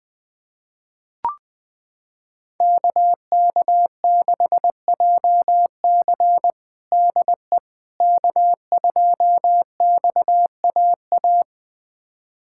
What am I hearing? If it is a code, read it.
Morse code, heard as a keyed tone, about 20 wpm, 700 Hz. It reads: KK6JC DE K2XAA
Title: USA Callsign DE Callsign - 5 Characters - 20wpm